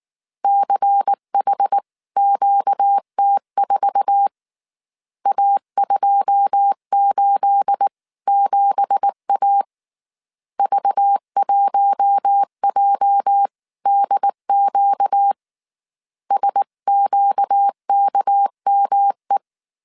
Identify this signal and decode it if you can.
Morse code, heard as a keyed tone, about 19 wpm, 800 Hz. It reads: CHQT4 A287A V1JDQ SQKME
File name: CW / DET / 1 9 1500C S A 18wpm 2459